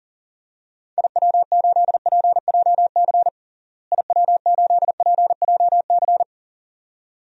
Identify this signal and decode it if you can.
Morse code, heard as a keyed tone, about 40 wpm, 700 Hz. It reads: IW8PJC IW8PJC
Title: DXCC-100 Call Signs - 6 Characters - Repeated 2x - 40wpm